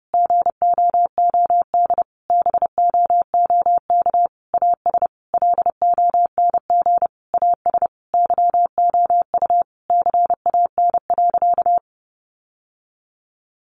Morse code, heard as a keyed tone, about 30 wpm, 700 Hz. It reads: GOOD BOOK AS LONG AS YOU CAN.
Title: Sentences from Top 1200 Words 30wpm